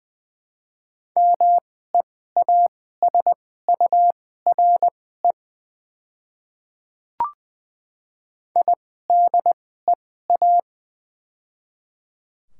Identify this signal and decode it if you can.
Morse code, heard as a keyed tone, about 20 wpm, spaced as for 15 wpm, 700 Hz. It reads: MEASURE IDEA
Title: 2x Character Spacing - Top 500 Words - 20wpm